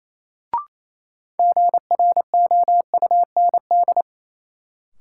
Morse code, heard as a keyed tone, about 28 wpm, 700 Hz. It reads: GROUND
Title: Sets of 3 Words - New Word Drill - 28wpm